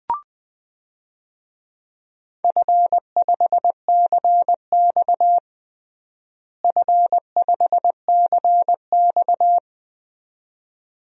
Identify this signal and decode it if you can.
Morse code, heard as a keyed tone, about 21 wpm, 700 Hz. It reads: F5CX F5CX
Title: DXCC-100 Call Signs - 4 Characters - Repeated 2x - 20wpm